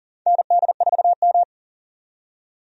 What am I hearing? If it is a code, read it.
Morse code, heard as a keyed tone, about 40 wpm, 700 Hz. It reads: ND4M
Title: USA Call Signs - 4 Characters - 40wpm